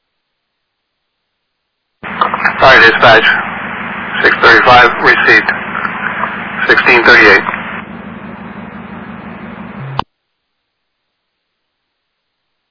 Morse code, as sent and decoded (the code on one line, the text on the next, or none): none